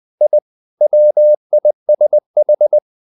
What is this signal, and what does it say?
Morse code, heard as a keyed tone, about 20 wpm, 600 Hz. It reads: I WISH
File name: the old man and the sea / the old man and the sea part 1 20wpm